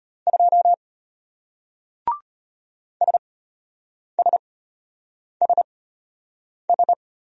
Morse code, heard as a keyed tone, about 33 wpm, spaced as for 8 wpm, 700 Hz. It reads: 2 SSSS